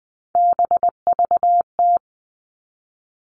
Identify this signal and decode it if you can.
Morse code, heard as a keyed tone, about 20 wpm, 700 Hz. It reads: BVT
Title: All Cut Numbers - 3-Digit 20wpm